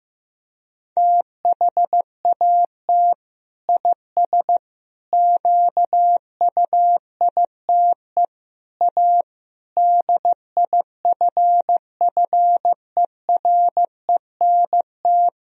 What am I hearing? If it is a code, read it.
Morse code, heard as a keyed tone, about 16 wpm, 700 Hz. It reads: THAT IS QUITE A DIFFERENT
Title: Sentences from Top 1000 Words - Review Part 3 - 15wpm